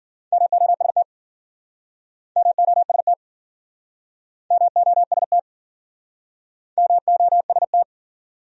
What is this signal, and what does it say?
Morse code, heard as a keyed tone, about 47 wpm, 700 Hz. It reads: MOST MOST MOST MOST
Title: Speed-Racing Top 100 Words 40wpm